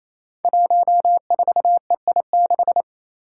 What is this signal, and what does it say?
Morse code, heard as a keyed tone, about 28 wpm, 700 Hz. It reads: 14EI6
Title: Five Challenging Letter-Number Combinations 28wpm